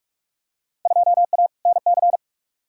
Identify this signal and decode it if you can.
Morse code, heard as a keyed tone, about 46 wpm, 700 Hz. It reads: 2A NC
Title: Field Day Exchanges - 45wpm